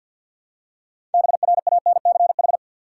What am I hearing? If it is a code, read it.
Morse code, heard as a keyed tone, about 53 wpm, 700 Hz. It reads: BRANCH
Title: Six Letter Words - 50wpm